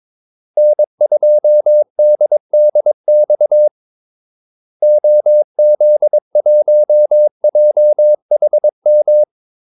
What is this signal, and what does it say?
Morse code, heard as a keyed tone, about 22 wpm, 600 Hz. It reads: N2DDX OZ1JHM
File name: Callsigns-22wpm